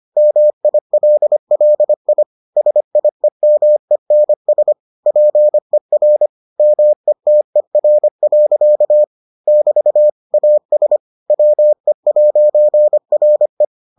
Morse code, heard as a keyed tone, about 25 wpm, 600 Hz. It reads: MILLI SIEMENS PER METER. <BT> AS WE'RE